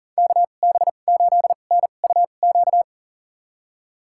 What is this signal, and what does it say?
Morse code, heard as a keyed tone, about 40 wpm, spaced as for 29 wpm, 700 Hz. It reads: KB8NUQ
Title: USA Call Signs - 6 Characters - 2x Character Spacing - 40wpm